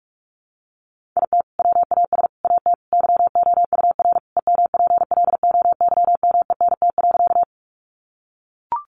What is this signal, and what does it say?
Morse code, heard as a keyed tone, about 44 wpm, 700 Hz. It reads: IT WAS AT YOUR EMPLOYMENT.
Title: Sentences from Top 1500 Words 45wpm